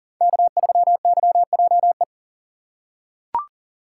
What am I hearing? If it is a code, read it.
Morse code, heard as a keyed tone, about 40 wpm, 700 Hz. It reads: K3YJE
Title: USA Call Signs - 5 Characters - 2x Character Spacing - 40wpm